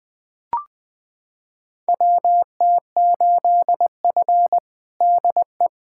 Morse code, heard as a keyed tone, about 20 wpm, 700 Hz. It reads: WT8F DE
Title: USA Callsign DE Callsign - 4 Characters - 20wpm